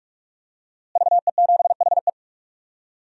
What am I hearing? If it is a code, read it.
Morse code, heard as a keyed tone, about 44 wpm, 700 Hz. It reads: VE7HE
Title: DXCC-100 Call Signs - 5 Characters - Repeated 2x - 45wpm